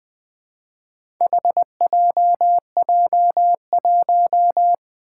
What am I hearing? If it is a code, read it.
Morse code, heard as a keyed tone, about 20 wpm, 700 Hz. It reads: HJJ1
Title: Four Challenging Letter-Number Combinations 20wpm